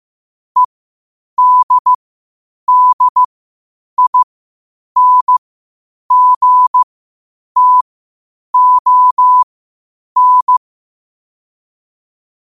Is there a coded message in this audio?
Yes: Morse code from 0.6 to 10.6 s, heard as a keyed tone, about 15 wpm, spaced as for 8 wpm, 1000 Hz. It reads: EDDINGTON